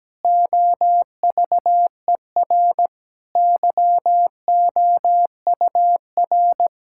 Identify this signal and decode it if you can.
Morse code, heard as a keyed tone, about 17 wpm, 700 Hz. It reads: OVER YOUR
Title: Pangram Challenge v2 - 17wpm